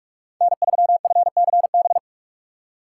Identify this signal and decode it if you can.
Morse code, heard as a keyed tone, about 58 wpm, 700 Hz. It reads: N3FCB